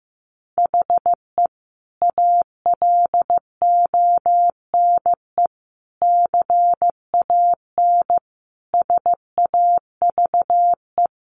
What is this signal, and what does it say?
Morse code, heard as a keyed tone, about 15 wpm, 700 Hz. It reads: HE ALONE CAN SAVE
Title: Sentences from Top 1100 Words 15wpm